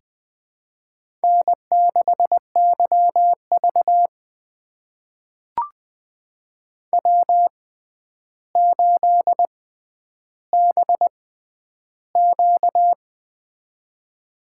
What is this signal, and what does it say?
Morse code, heard as a keyed tone, about 20 wpm, spaced as for 10 wpm, 700 Hz. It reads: N6YV W8BQ
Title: USA Call Signs - 4 Characters - 6x Character Spacing - 20wpm